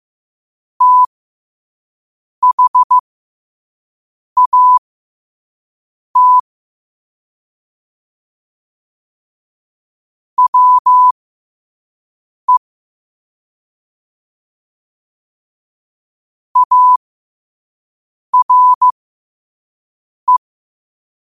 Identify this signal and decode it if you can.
Morse code, heard as a keyed tone, about 15 wpm, spaced as for 5 wpm, 1000 Hz. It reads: THAT WE ARE
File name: qotdfiles / 5wpm